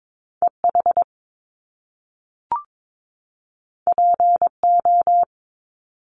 Morse code, heard as a keyed tone, about 22 wpm, 700 Hz. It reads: EH PO
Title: Top Two Letter Combinations in Words 22wpm